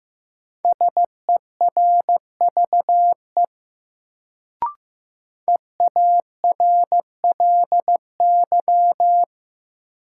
Morse code, heard as a keyed tone, about 15 wpm, 700 Hz. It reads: SERVE EARLY
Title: Sentences from Top 400 Words - New Word Drill - 15wpm